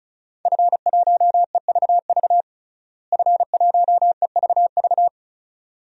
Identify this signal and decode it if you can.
Morse code, heard as a keyed tone, about 34 wpm, 700 Hz. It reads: F1EVV F1EVV
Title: DXCC-100 Call Signs - 5 Characters - Repeated 2x - 35wpm